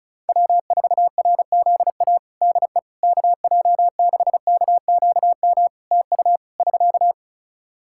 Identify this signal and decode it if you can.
Morse code, heard as a keyed tone, about 34 wpm, 700 Hz. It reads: W4RZA DE KJ6KQM TU <SK>
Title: Practice QSO - Exchange 4 - Ending - 35wpm